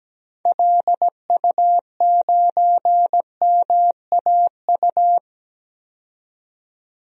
Morse code, heard as a keyed tone, about 17 wpm, 700 Hz. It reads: LU9MAU
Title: DXCC-100 Call Signs - 6 Characters - 17wpm